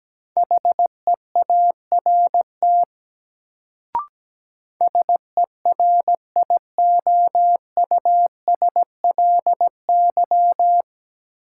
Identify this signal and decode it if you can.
Morse code, heard as a keyed tone, about 17 wpm, 700 Hz. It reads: HEART SERIOUSLY
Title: Sentences from Top 1200 Words - New Word Drill Alt - 17wpm